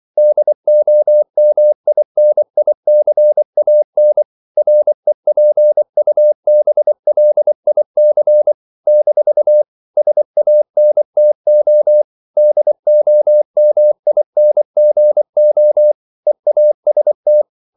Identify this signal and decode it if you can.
Morse code, heard as a keyed tone, about 24 wpm, 600 Hz. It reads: DOMINICAN REPUBLIC - SANTO DOMINGO EAST
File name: CW / Practice / NationalCapitals24